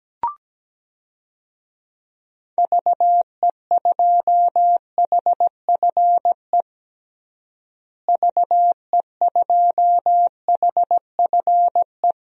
Morse code, heard as a keyed tone, about 17 wpm, 700 Hz. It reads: VE2HFE VE2HFE
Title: DXCC-100 Call Signs - 6 Characters - Repeated 2x - 17wpm